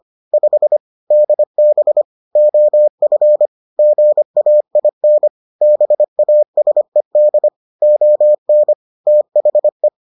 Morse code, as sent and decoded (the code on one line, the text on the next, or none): ..... / -.. -... / --- ..-. / --. .- .. -. / -... .- ... -.. / --- -. / - .... .
5 DB OF GAIN BASED ON THE